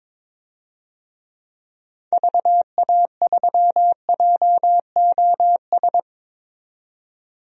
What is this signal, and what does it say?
Morse code, heard as a keyed tone, about 22 wpm, 700 Hz. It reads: VA3JOS
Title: DXCC-100 Call Signs - 6 Characters - 22wpm